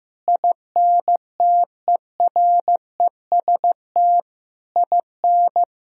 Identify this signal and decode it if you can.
Morse code, heard as a keyed tone, about 15 wpm, 700 Hz. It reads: INTEREST IN
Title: Sentences from Top 1000 Words - Review Part 3 - 15wpm